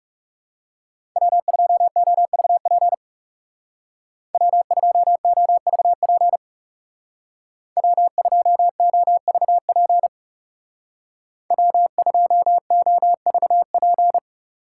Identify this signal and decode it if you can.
Morse code, heard as a keyed tone, about 39 wpm, 700 Hz. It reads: W2OVP W2OVP W2OVP W2OVP